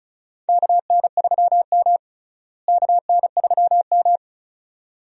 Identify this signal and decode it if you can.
Morse code, heard as a keyed tone, about 34 wpm, 700 Hz. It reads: KN3M KN3M
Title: USA Call Signs - 4 Characters - Repeated 2x - 35wpm